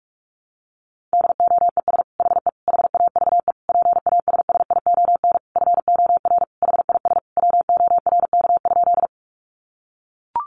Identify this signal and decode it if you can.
Morse code, heard as a keyed tone, about 44 wpm, 700 Hz. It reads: DOES HE HAVE PASSION FOR HIS WORK?
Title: Sentences from Top 1300 Words 45wpm